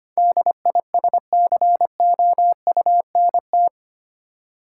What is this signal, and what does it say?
Morse code, heard as a keyed tone, about 25 wpm, 700 Hz. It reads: DISCOUNT
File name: 8 Letter Words - First 3 Letters Given - 25wpm